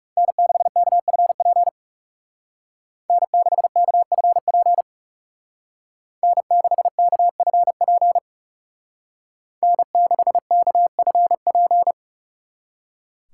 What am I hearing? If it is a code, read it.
Morse code, heard as a keyed tone, about 36 wpm, 700 Hz. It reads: N6KFP N6KFP N6KFP N6KFP
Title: Speed Racing - USA Call Signs 30wpm